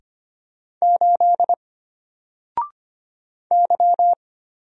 Morse code, heard as a keyed tone, about 25 wpm, 700 Hz. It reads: 8 Y
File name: Single-Letter-Number-25wpm